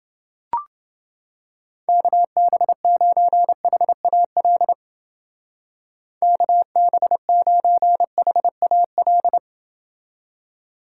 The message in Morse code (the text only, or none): KB9HAL KB9HAL